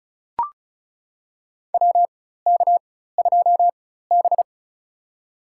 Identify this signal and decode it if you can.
Morse code, heard as a keyed tone, about 36 wpm, spaced as for 16 wpm, 700 Hz. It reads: WK2B
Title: USA Call Signs - 4 Characters - 4x Character Spacing - 35wpm